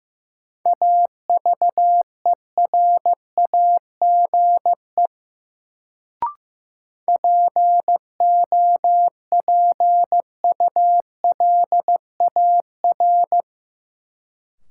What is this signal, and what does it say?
Morse code, heard as a keyed tone, about 15 wpm, 700 Hz. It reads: AVERAGE POPULAR
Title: Sets of 5 Words - New Word Drill - 15wpm